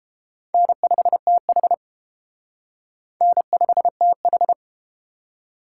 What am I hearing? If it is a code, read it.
Morse code, heard as a keyed tone, about 31 wpm, 700 Hz. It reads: N5TH N5TH